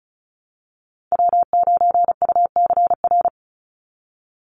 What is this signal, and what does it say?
Morse code, heard as a keyed tone, about 34 wpm, 700 Hz. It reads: W9UCR
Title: Call Signs - 35wpm